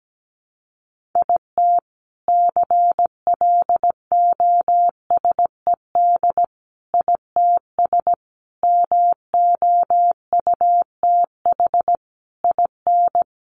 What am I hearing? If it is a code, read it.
Morse code, heard as a keyed tone, about 17 wpm, 700 Hz. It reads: IT CLOSED ITS MOUTH IN
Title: A Sound Of Thunder 17wpm